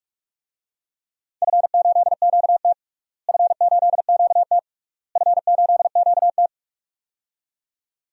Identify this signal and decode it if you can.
Morse code, heard as a keyed tone, about 42 wpm, 700 Hz. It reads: F8QT F8QT F8QT
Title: DXCC-100 Call Signs - 4 Characters - Repeated 3x - 45wpm